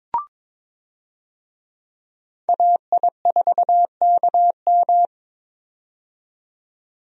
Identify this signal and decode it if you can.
Morse code, heard as a keyed tone, about 22 wpm, 700 Hz. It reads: AI4KM
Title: USA Call Signs - 5 Characters - 22wpm